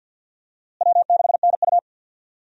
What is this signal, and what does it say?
Morse code, heard as a keyed tone, about 51 wpm, 700 Hz. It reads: W6NU
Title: USA Call Signs - 4 Characters - Repeated 3x - 50wpm